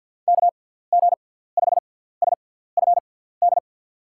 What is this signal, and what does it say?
Morse code, heard as a keyed tone, about 49 wpm, spaced as for 17 wpm, 700 Hz. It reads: KG5SFD
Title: USA Call Signs - 6 Characters - 6x Character Spacing - 50wpm